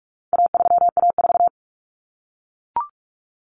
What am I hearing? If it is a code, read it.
Morse code, heard as a keyed tone, about 44 wpm, 700 Hz. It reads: A3A4
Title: Common Cut Numbers - 4-Digit 45wpm